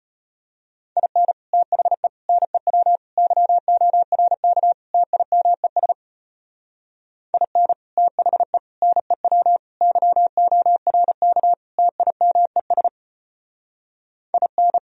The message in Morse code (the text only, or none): IN THE NEW YORK TIMES IN THE NEW YORK TIMES IN